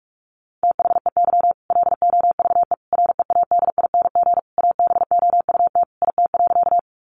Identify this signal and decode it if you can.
Morse code, heard as a keyed tone, about 44 wpm, 700 Hz. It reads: THEY LOVE READING ABOUT IT.
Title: Sentences from Top 600 Words 45wpm